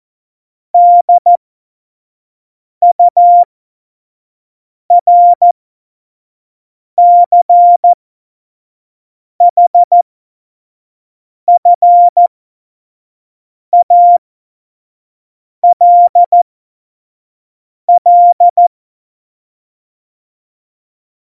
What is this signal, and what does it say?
Morse code, heard as a keyed tone, about 14 wpm, spaced as for 5 wpm, 700 Hz. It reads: DURCHFALL
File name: Morsekurs / Witze / Witz 30. Januar 2018, 25 F70